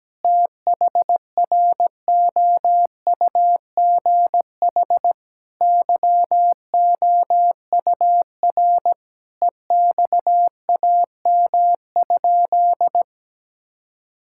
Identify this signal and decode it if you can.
Morse code, heard as a keyed tone, about 17 wpm, 700 Hz. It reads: THROUGH YOUR EXAM?
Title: Sentences from Top 1000 Words - Review Part 5 - 17wpm